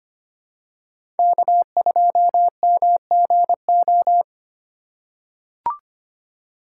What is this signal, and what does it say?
Morse code, heard as a keyed tone, about 25 wpm, 700 Hz. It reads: K2MGO
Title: USA Call Signs - 5 Characters - 3x Character Spacing - 25wpm